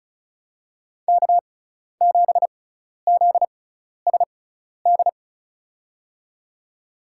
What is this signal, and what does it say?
Morse code, heard as a keyed tone, about 34 wpm, spaced as for 12 wpm, 700 Hz. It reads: K7ZSD
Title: USA Call Signs - 5 Characters - 6x Character Spacing - 35wpm